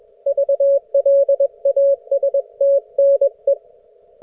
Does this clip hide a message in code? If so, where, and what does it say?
Yes: Morse code from 0.3 to 3.5 s, heard as a keyed tone, about 21 wpm, 560 Hz. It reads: VLASTNE